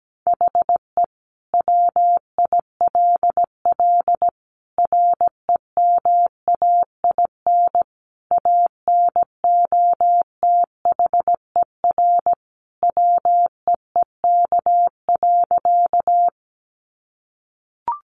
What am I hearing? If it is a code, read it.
Morse code, heard as a keyed tone, about 17 wpm, 700 Hz. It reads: HE WILL REMAIN ANOTHER WEEK.
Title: Sentences from Top 1200 Words 17wpm